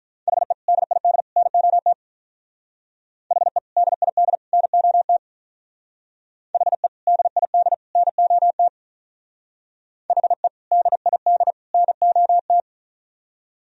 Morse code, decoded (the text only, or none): HE DID NOT HE DID NOT HE DID NOT HE DID NOT